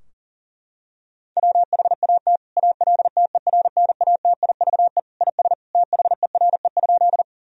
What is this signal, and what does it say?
Morse code, heard as a keyed tone, about 40 wpm, 700 Hz. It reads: WHAT ALTERNATIVE IS THERE?
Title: Sentences from Top 1000 Words - Review Part 4 - 40wpm